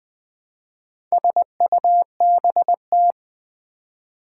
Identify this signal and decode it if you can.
Morse code, heard as a keyed tone, about 20 wpm, 700 Hz. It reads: SUBT
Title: Build A Word - 2 Letters At A Time - 6-Letter Words - 20wpm